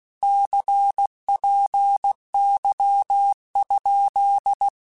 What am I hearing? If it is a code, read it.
Morse code, heard as a keyed tone, about 16 wpm, 800 Hz. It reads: CPY?